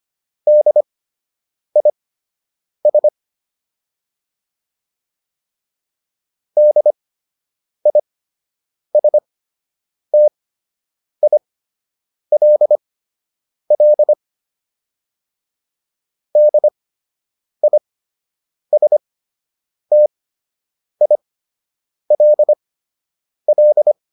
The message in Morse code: DIS DISTILL DISTILL

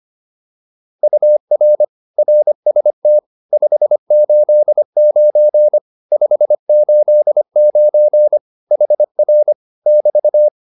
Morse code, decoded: UR RST 589 589 HR <BT>

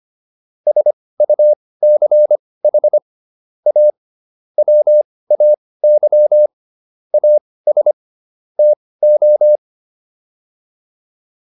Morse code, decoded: SUCH A WAY AS TO

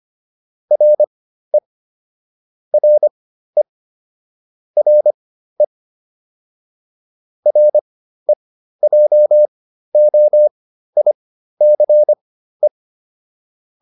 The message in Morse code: RE RE RE REJOICE

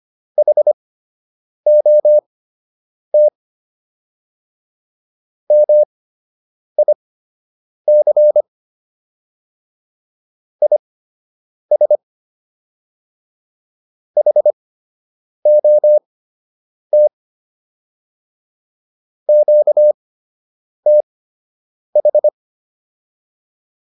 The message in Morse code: HOT MIC IS HOT QTH

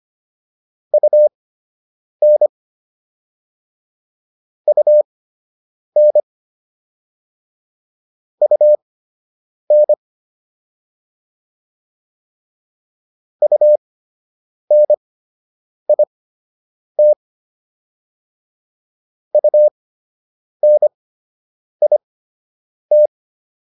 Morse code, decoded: UN UN UN UNIT UNIT

